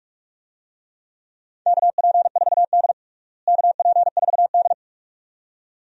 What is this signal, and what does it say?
Morse code, heard as a keyed tone, about 46 wpm, 700 Hz. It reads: KP4D KP4D
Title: DXCC-100 Call Signs - 4 Characters - Repeated 2x - 45wpm